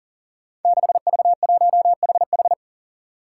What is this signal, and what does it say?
Morse code, heard as a keyed tone, about 41 wpm, 700 Hz. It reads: 6V1HH
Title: Five Challenging Letter-Number Combinations 40wpm